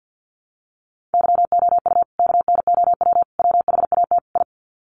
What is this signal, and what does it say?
Morse code, heard as a keyed tone, about 48 wpm, 700 Hz. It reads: YOU KNOW WHAT I
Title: Sets of 4 Words 50wpm